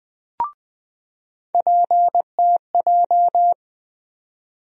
Morse code, heard as a keyed tone, about 21 wpm, 700 Hz. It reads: PTJ
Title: USA Call Sign Postfixes - 3 Letters 20wpm